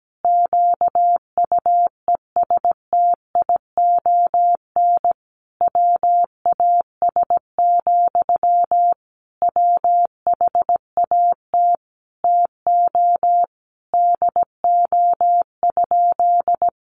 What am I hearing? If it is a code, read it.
Morse code, heard as a keyed tone, about 17 wpm, 700 Hz. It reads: QUESTION WAS, WHAT TO DO?
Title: Sentences from Top 400 Words 17wpm